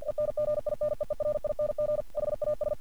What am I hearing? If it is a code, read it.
Morse code, heard as a keyed tone, about 48 wpm, 600 Hz. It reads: ENGINEERING HAS
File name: archived / CRSnet